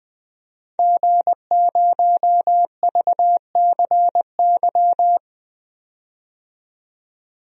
Morse code, heard as a keyed tone, about 20 wpm, 700 Hz. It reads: G0VCY
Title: DXCC-100 Call Signs - 5 Characters - 20wpm